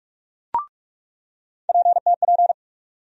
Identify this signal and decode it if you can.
Morse code, heard as a keyed tone, about 42 wpm, 700 Hz. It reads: PTP
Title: USA Call Sign Postfixes - 3 Letters 45wpm